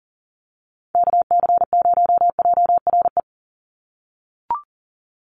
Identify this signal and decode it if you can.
Morse code, heard as a keyed tone, about 40 wpm, 700 Hz. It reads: KC0JRE
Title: USA Call Signs 40wpm